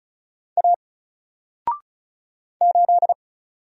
Morse code, heard as a keyed tone, about 35 wpm, 700 Hz. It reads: A 8
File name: Single-Letter-Number-35wpm